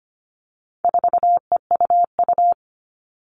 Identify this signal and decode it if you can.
Morse code, heard as a keyed tone, about 25 wpm, 700 Hz. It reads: 4EUU
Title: All Cut Numbers - 4-Digit 25wpm